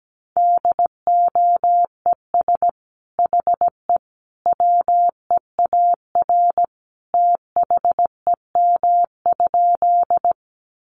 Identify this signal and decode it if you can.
Morse code, heard as a keyed tone, about 17 wpm, 700 Hz. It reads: DOES HE WEAR THEM?
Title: Sentences from Top 1500 Words 17wpm